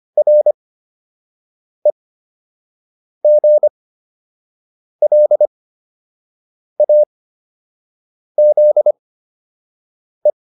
Morse code, heard as a keyed tone, about 25 wpm, spaced as for 6 wpm, 600 Hz. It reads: REGLAZE
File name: RE7 06